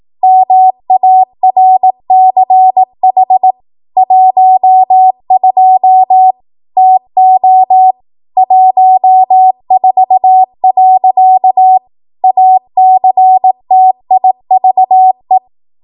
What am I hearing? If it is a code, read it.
Morse code, heard as a keyed tone, about 18 wpm, 750 Hz. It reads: MARCH 12 TO 14. ACTIVE